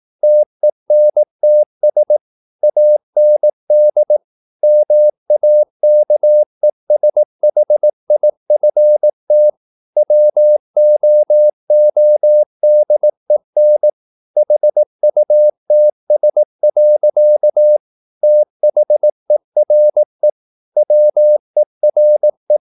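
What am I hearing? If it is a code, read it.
Morse code, heard as a keyed tone, about 18 wpm, 600 Hz. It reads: TENTS AND MAKESHIFT WOODEN HUTS. THERE WERE
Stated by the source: Morse practice at 18 wpm, English text, from CW med Gnister.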